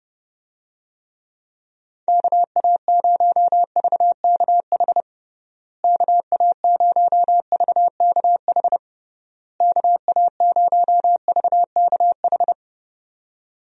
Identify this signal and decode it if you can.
Morse code, heard as a keyed tone, about 30 wpm, 700 Hz. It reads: KA0VKH KA0VKH KA0VKH
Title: USA Call Signs - 6 Characters - Repeated 3x - 30wpm